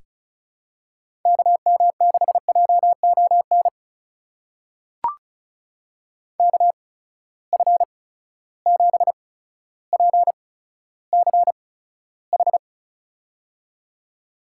Morse code, heard as a keyed tone, about 36 wpm, spaced as for 15 wpm, 700 Hz. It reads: KM6JON KF7PCH
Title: USA Call Signs - 6 Characters - 8x Character Spacing - 35wpm